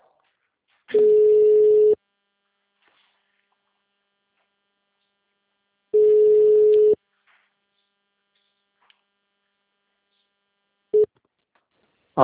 Ringback tone (Europe, 425 Hz).
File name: out- -2025- - - .9